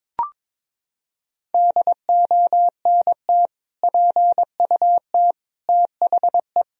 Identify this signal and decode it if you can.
Morse code, heard as a keyed tone, about 22 wpm, 700 Hz. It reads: DONT PUT THE
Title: Sentences from Top 1000 Words - Review Part 2 - 22wpm